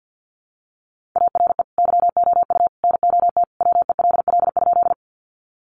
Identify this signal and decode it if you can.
Morse code, heard as a keyed tone, about 48 wpm, 700 Hz. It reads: ARE YOU NOT WELL?
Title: Sentences from Top 600 Words 50wpm